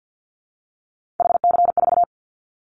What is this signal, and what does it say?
Morse code, heard as a keyed tone, about 50 wpm, 700 Hz. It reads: HC4
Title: Callsign Prefixes for Top 100 DXCC - 50wpm